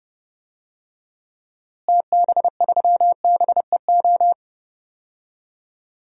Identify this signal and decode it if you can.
Morse code, heard as a keyed tone, about 30 wpm, 700 Hz. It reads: TB3BEO
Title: DXCC-100 Call Signs - 6 Characters - Repeated 2x - 30wpm